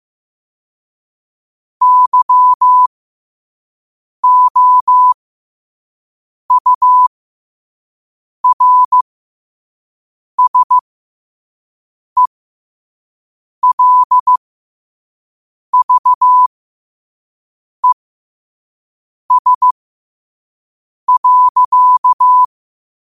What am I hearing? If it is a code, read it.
Morse code, heard as a keyed tone, about 15 wpm, spaced as for 5 wpm, 1000 Hz. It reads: YOURSELVES.
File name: qotdfiles / 5wpm